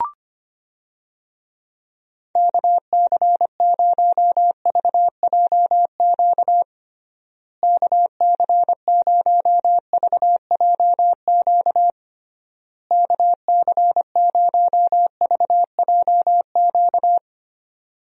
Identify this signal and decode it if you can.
Morse code, heard as a keyed tone, about 25 wpm, 700 Hz. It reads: KC0VJQ KC0VJQ KC0VJQ
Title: USA Call Signs - 6 Characters - Repeated 3x - 25wpm